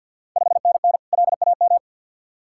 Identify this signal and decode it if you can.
Morse code, heard as a keyed tone, about 50 wpm, 690 Hz. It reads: 5NN LAM
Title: Ontario QSO Party Exchange 50wpm